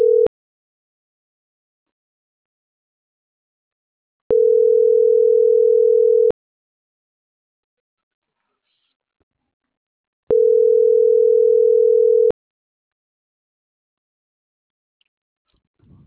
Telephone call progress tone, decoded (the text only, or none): ringback tone